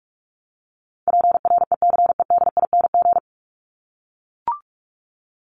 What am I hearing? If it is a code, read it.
Morse code, heard as a keyed tone, about 44 wpm, 700 Hz. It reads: PRECEDING
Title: ING Suffix - Top 100 Words with Base Words 45wpm